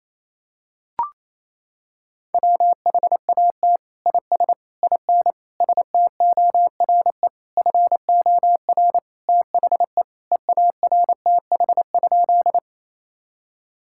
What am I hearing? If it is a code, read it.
Morse code, heard as a keyed tone, about 27 wpm, 700 Hz. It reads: WHAT IS IN STORE FOR THE EARTH?
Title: Sentences from Top 1000 Words - Review Part 5 - 28wpm